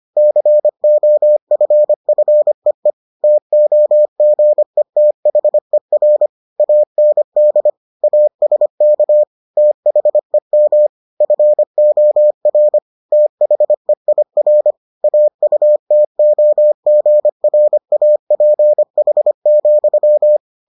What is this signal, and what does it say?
Morse code, heard as a keyed tone, about 25 wpm, 600 Hz. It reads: COFFEE TOGETHER AND ASK THEM FOR THEIR AUTOGRAPH,